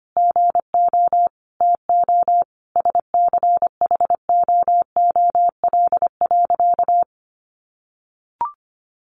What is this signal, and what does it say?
Morse code, heard as a keyed tone, about 25 wpm, 700 Hz. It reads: GO TO SCHOOL.
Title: Easy Phrases 25wpm